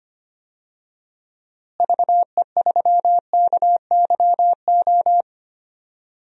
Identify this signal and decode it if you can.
Morse code, heard as a keyed tone, about 25 wpm, 700 Hz. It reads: VE3KYO
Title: DXCC-100 Call Signs - 6 Characters - Repeated 2x - 25wpm